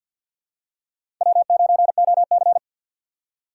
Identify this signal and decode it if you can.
Morse code, heard as a keyed tone, about 50 wpm, 700 Hz. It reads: W9OC
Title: USA Call Signs - 4x Character Spacing - 50wpm